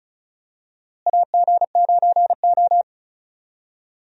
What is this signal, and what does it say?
Morse code, heard as a keyed tone, about 36 wpm, 700 Hz. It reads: AG9O
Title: USA Call Signs - 4 Characters - Repeated 2x - 35wpm